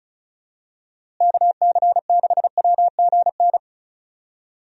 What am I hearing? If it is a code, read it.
Morse code, heard as a keyed tone, about 36 wpm, 700 Hz. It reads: KC6WGN
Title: USA Call Signs - 6 Characters - Repeated 2x - 35wpm